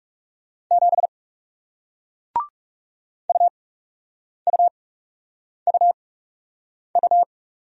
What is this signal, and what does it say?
Morse code, heard as a keyed tone, about 40 wpm, spaced as for 8 wpm, 700 Hz. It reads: 7 UUUU